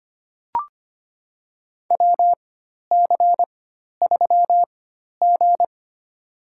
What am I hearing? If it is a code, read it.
Morse code, heard as a keyed tone, about 25 wpm, spaced as for 12 wpm, 700 Hz. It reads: WC3G
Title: USA Call Signs - 4 Characters - 4x Character Spacing - 25wpm